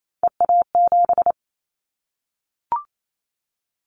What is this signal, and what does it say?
Morse code, heard as a keyed tone, about 28 wpm, 700 Hz. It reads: EA7
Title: Callsign Prefixes for Top 25 DXCC - Repeated 3x - 28wpm